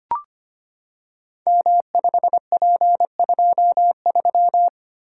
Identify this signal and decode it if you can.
Morse code, heard as a keyed tone, about 25 wpm, 700 Hz. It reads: M5P23
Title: Code Groups - 5 Letter:Numbers - 25wpm